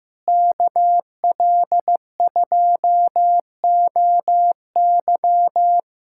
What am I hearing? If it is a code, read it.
Morse code, heard as a keyed tone, about 15 wpm, 700 Hz. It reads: KL2OY
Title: USA Callsign DE Callsign 15wpm